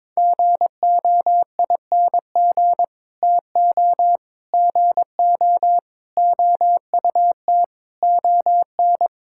Morse code, heard as a keyed tone, about 22 wpm, 700 Hz. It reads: GOING TO GO OUT ON
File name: practice / 22wpm / Sets of 8 Words - 22wpm